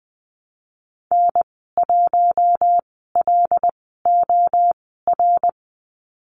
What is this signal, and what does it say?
Morse code, heard as a keyed tone, about 19 wpm, spaced as for 14 wpm, 700 Hz. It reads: N1LOR
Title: 076 Lesson 13g - Letter L - Call Signs 20wpm